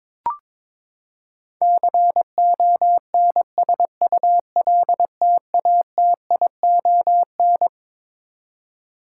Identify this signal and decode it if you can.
Morse code, heard as a keyed tone, about 21 wpm, 700 Hz. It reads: CONSULTATION
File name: Twelve Letter Words - 22wpm